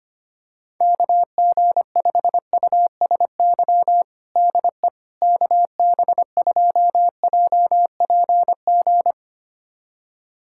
Morse code, heard as a keyed tone, about 25 wpm, 700 Hz. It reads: KG5USY DE KB2JPG
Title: USA Callsign DE Callsign - 6 Characters - 25wpm